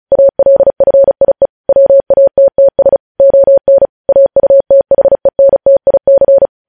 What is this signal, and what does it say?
Morse code, heard as a keyed tone, about 36 wpm, 550 Hz. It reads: ALFIE WATTS ON AUTHENTIC
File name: podcast / News headlines-1x-35x35